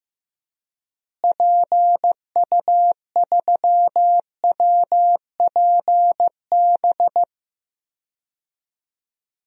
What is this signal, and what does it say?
Morse code, heard as a keyed tone, about 15 wpm, 700 Hz. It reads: PU3WPB
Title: DXCC-100 Call Signs - 6 Characters - Repeated 3x - 15wpm